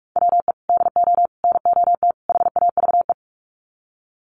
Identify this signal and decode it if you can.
Morse code, heard as a keyed tone, about 44 wpm, 700 Hz. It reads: WE DO NOT HAVE
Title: Sets of 4 Words 45wpm